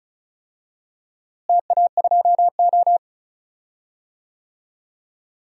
Morse code, heard as a keyed tone, about 34 wpm, 700 Hz. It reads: TA2O
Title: DXCC-100 Call Signs - 4 Characters - Repeated 2x - 35wpm